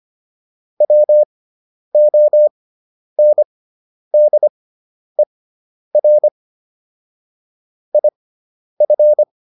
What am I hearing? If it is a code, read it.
Morse code, heard as a keyed tone, about 25 wpm, spaced as for 10 wpm, 600 Hz. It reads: WONDER IF